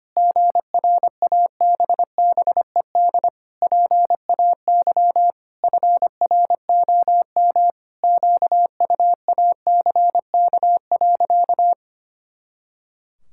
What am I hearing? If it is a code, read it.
Morse code, heard as a keyed tone, about 25 wpm, 700 Hz. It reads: GRABBED PAY FROM QUACK.
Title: Pangram Challenge v2 - 25wpm